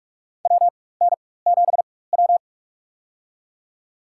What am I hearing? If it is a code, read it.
Morse code, heard as a keyed tone, about 43 wpm, spaced as for 21 wpm, 700 Hz. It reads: WN7W
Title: USA Call Signs - 4x Character Spacing - 45wpm